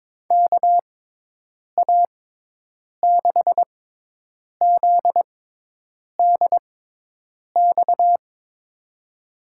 Morse code, heard as a keyed tone, about 21 wpm, spaced as for 8 wpm, 700 Hz. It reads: KA6ZDX
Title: USA Call Signs - 6 Characters - 6x Character Spacing - 22wpm